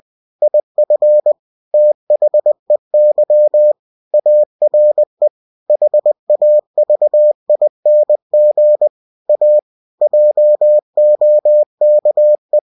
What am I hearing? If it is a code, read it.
Morse code, heard as a keyed tone, about 20 wpm, 600 Hz. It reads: IF THEY ARE HAVING A JOKE
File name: cw-books / winnie-the-pooh / winnie-20wpm-page-0071